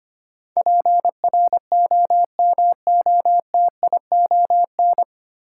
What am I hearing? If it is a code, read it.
Morse code, heard as a keyed tone, about 25 wpm, 700 Hz. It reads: PROMOTION
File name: Nine Letter Words - 25wpm